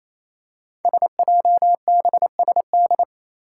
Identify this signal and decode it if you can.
Morse code, heard as a keyed tone, about 29 wpm, 700 Hz. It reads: SJBSD
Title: Five Challenging Letter-Number Combinations 28wpm